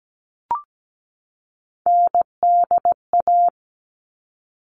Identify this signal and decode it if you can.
Morse code, heard as a keyed tone, about 17 wpm, 700 Hz. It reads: NDA
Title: All Cut Numbers - 3-Digit 17wpm